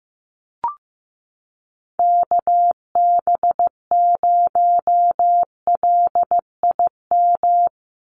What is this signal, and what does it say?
Morse code, heard as a keyed tone, about 15 wpm, 700 Hz. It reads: KB0LIM
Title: USA Call Signs 15wpm